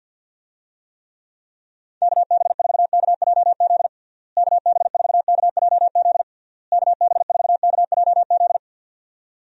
Morse code, heard as a keyed tone, about 52 wpm, 700 Hz. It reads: KB4KJZ KB4KJZ KB4KJZ
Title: USA Call Signs - 6 Characters - Repeated 3x - 50wpm